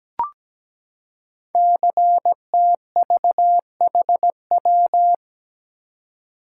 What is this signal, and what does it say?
Morse code, heard as a keyed tone, about 17 wpm, 700 Hz. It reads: CTVHW
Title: Code Groups - 5 Letters - 17wpm